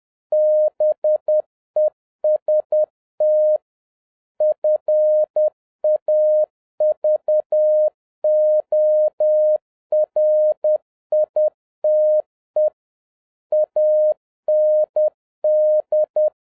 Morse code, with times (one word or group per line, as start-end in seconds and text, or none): 0.3-3.6 BEST
4.4-12.7 FAVORITE
13.5-16.3 AND